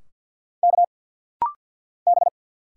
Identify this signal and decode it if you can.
Morse code, heard as a keyed tone, about 50 wpm, 700 Hz. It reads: K B
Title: Single Letters - Rapid-Fire 50wpm